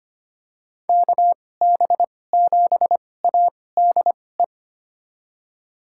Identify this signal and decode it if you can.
Morse code, heard as a keyed tone, about 25 wpm, spaced as for 18 wpm, 700 Hz. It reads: KB7ADE
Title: USA Call Signs - 6 Characters - 2x Character Spacing - 25wpm